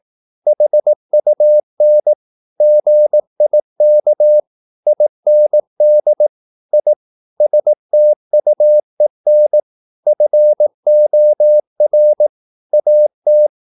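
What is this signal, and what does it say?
Morse code, heard as a keyed tone, about 18 wpm, 600 Hz. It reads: HUN GIK IND I STUEN FOR AT
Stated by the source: Slow (QRS) Morse practice at 18 wpm, CW med Gnister.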